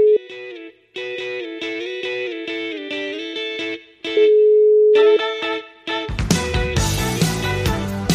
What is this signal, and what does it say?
Ringback tone (Europe, 425 Hz), with noise.